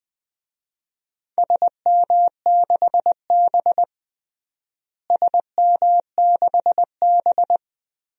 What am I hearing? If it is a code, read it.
Morse code, heard as a keyed tone, about 21 wpm, 700 Hz. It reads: SM6B SM6B
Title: DXCC-100 Call Signs - 4 Characters - Repeated 3x - 20wpm